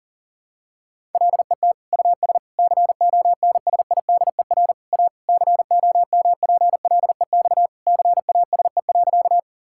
Morse code, heard as a keyed tone, about 40 wpm, 700 Hz. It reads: LET US CONSIDER A COMPLEX CASE.
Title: Sentences from Top 1000 Words - Review Part 6 - 40wpm